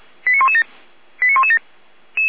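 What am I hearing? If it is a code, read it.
Five-tone selective call (ZVEI), heard as quick runs of five tones, tones of about 80 ms. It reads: 89117 89117
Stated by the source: ZVEI five-tone selective call received on SDR.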